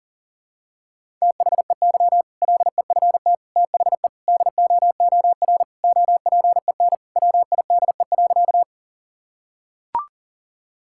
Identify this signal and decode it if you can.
Morse code, heard as a keyed tone, about 40 wpm, 700 Hz. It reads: THEY LEFT THE DOOR OPEN WIDE.
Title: Sentences from Top 1000 Words - Review Part 6 - 40wpm